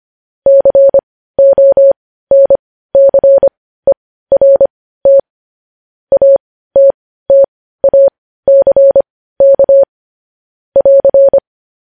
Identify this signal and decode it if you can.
Morse code, heard as a keyed tone, about 25 wpm, spaced as for 15 wpm, 550 Hz. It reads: CONCERT ATTACK <AR>
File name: podcast / News headlines-1x-25x15